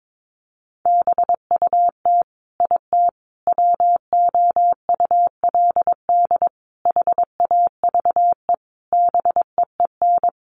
Morse code, heard as a keyed tone, about 22 wpm, 700 Hz. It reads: BUT IT WOULD HAVE BEEN
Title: Sentences from Top 1500 Words 22wpm